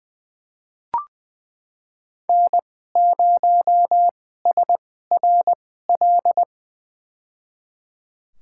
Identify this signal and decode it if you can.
Morse code, heard as a keyed tone, about 20 wpm, spaced as for 15 wpm, 700 Hz. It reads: N0SRL
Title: USA Call Signs - 5 Characters - 2x Character Spacing - 20wpm